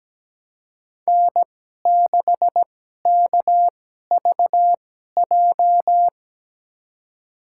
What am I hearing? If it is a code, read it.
Morse code, heard as a keyed tone, about 17 wpm, spaced as for 12 wpm, 700 Hz. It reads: N6KVJ
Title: USA Call Signs - 2x Character Spacing - 17wpm